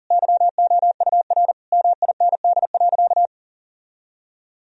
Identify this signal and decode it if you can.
Morse code, heard as a keyed tone, about 40 wpm, 700 Hz. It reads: YOUR MIND.
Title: Sentences from Top 1000 Words - Review Part 3 - 40wpm